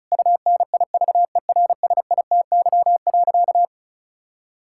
Morse code, heard as a keyed tone, about 34 wpm, 700 Hz. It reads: UNIVERSITY.
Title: Sentences from Top 1000 Words - Review Part 5 - 35wpm